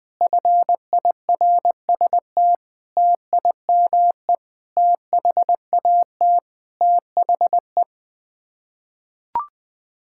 Morse code, heard as a keyed tone, about 21 wpm, 700 Hz. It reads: FIRST TIME THAT THE
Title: Sets of 8 Words - 20wpm